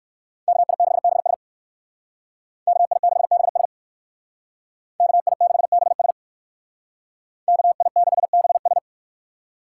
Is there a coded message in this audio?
Yes: Morse code from 0.5 to 8.8 s, heard as a keyed tone, about 58 wpm, 700 Hz. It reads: KI6BS KI6BS KI6BS KI6BS